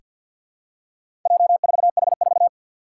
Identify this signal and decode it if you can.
Morse code, heard as a keyed tone, about 47 wpm, 700 Hz. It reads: J4H4
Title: Four Challenging Letter-Number Combinations 50wpm